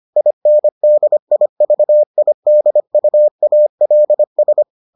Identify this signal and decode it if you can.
Morse code, heard as a keyed tone, about 25 wpm, 600 Hz. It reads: INDIVIDUALS